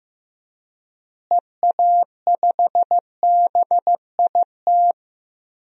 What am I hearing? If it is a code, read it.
Morse code, heard as a keyed tone, about 15 wpm, 700 Hz. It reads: EA5BIT
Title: DXCC-100 Call Signs - 6 Characters - Repeated 2x - 15wpm